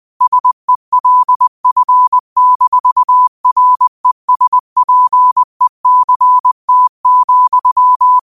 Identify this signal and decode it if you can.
Morse code, heard as a keyed tone, about 20 wpm, 1000 Hz. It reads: SELF-RESPECT,
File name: qotdfiles / 20wpm